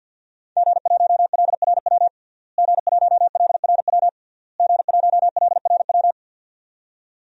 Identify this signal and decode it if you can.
Morse code, heard as a keyed tone, about 46 wpm, 700 Hz. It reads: G1LRW G1LRW G1LRW
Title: DXCC-100 Call Signs - 5 Characters - Repeated 3x - 50wpm